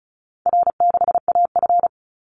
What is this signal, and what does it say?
Morse code, heard as a keyed tone, about 34 wpm, 700 Hz. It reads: R6AF
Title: Easy Call Signs Repeated 2x - 35wpm